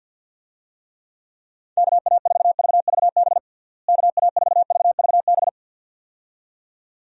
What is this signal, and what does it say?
Morse code, heard as a keyed tone, about 48 wpm, 700 Hz. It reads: KA4VVB KA4VVB
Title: USA Call Signs - 6 Characters - Repeated 2x - 50wpm